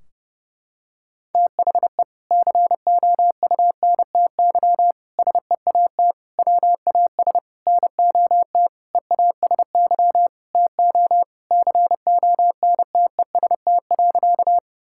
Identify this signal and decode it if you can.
Morse code, heard as a keyed tone, about 30 wpm, 700 Hz. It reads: THE COUNTY SEAT WAS NOT EASY TO CONTEST.
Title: Sentences from Top 1600 Words 30wpm